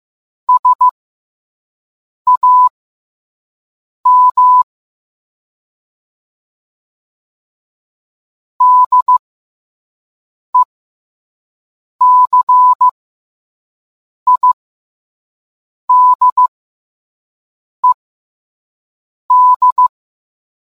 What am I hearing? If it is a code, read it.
Morse code, heard as a keyed tone, about 15 wpm, spaced as for 5 wpm, 1000 Hz. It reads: SAM DECIDED